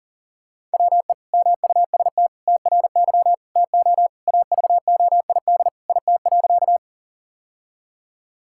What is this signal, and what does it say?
Morse code, heard as a keyed tone, about 40 wpm, 700 Hz. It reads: WE MUST TRY TO AVOID IT.
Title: Sentences from Top 1000 Words - Review Part 6 - 40wpm